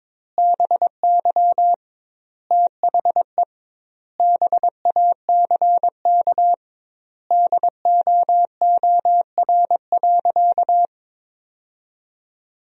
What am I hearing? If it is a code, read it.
Morse code, heard as a keyed tone, about 22 wpm, 700 Hz. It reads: BY THE BACK DOOR.
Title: 2x Word-Spacing - Sentences from Top 300 Words - 22wpm